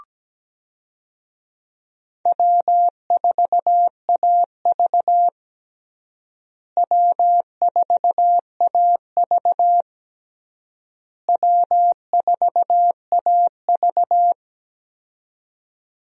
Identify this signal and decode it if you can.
Morse code, heard as a keyed tone, about 17 wpm, 700 Hz. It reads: W4AV W4AV W4AV
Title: USA Call Signs - 4 Characters - Repeated 3x - 17wpm